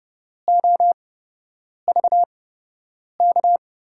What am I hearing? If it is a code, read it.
Morse code, heard as a keyed tone, about 32 wpm, spaced as for 8 wpm, 700 Hz. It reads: OVK
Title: USA Call Signs - 5 Characters - 8x Character Spacing - 30wpm